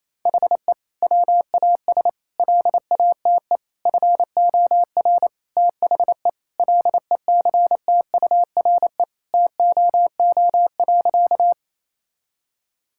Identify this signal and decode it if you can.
Morse code, heard as a keyed tone, about 28 wpm, 700 Hz. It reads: HE WAS LATE FOR THE LECTURE TOO.
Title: Sentences from Top 1600 Words 28wpm